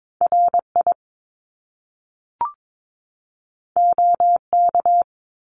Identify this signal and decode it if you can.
Morse code, heard as a keyed tone, about 22 wpm, 700 Hz. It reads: RI OK
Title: Top Two Letter Combinations in Words 22wpm